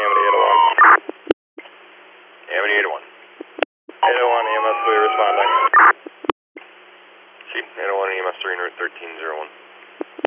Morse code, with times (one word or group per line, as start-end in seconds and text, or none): none